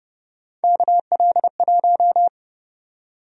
Morse code, heard as a keyed tone, about 31 wpm, 700 Hz. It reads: KL1
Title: USA Call Sign Prefixes - 30wpm